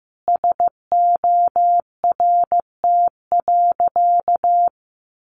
Morse code, heard as a keyed tone, about 15 wpm, 700 Hz. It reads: SORT.
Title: Sentences from Top 1300 Words 15wpm